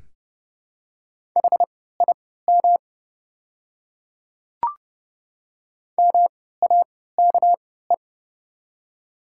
Morse code, heard as a keyed tone, about 30 wpm, spaced as for 17 wpm, 700 Hz. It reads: HIM MAKE
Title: 3x Character Spacing - Top 100 Words - 30wpm